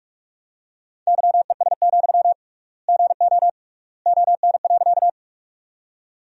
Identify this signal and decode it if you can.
Morse code, heard as a keyed tone, about 46 wpm, 700 Hz. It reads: YES, GO ON.
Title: 3x Word-Spacing - Sentences from Top 500 Words - 45wpm